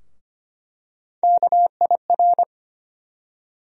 Morse code, heard as a keyed tone, about 25 wpm, 700 Hz. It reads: KIR
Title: USA Call Sign Postfixes - 3 Letters 25wpm